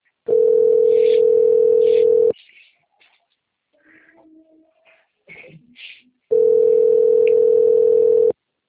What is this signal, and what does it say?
Ringback tone (North America), 440 + 480 Hz.
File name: rg-1600-unknown- - - .42323